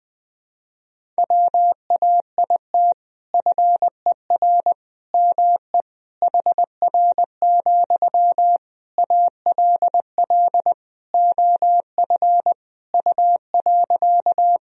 Morse code, heard as a keyed tone, about 20 wpm, 700 Hz. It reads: WAIT FER ME HR, ALL OF U.
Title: Sentences from Top 500 Words and Abbreviations 20wpm